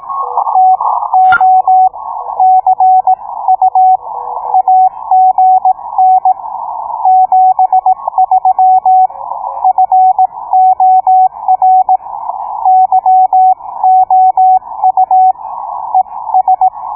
Morse code, heard as a keyed tone, about 17 wpm, spaced as for 15 wpm, 750 Hz. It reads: TO CU AGN 73 FOR YOU ES